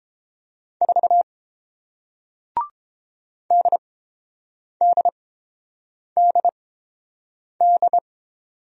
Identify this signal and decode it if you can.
Morse code, heard as a keyed tone, about 30 wpm, spaced as for 7 wpm, 700 Hz. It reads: 4 DDDD